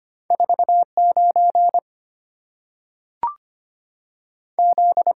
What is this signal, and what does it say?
Morse code, heard as a keyed tone, about 25 wpm, 690 Hz. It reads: 49 Z
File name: Two Letter-Number Combinations - 25wpm